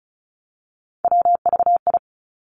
Morse code, heard as a keyed tone, about 36 wpm, 700 Hz. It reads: WVI
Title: Three Challenging Letter-Number Combinations 35wpm